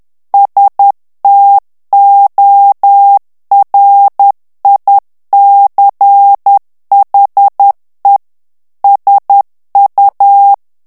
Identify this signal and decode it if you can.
Morse code, heard as a keyed tone, about 10 wpm, 800 Hz. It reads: STORICHE SU
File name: morse mp3 / ITALIEN texte clair 1 vitesse 10 mots minute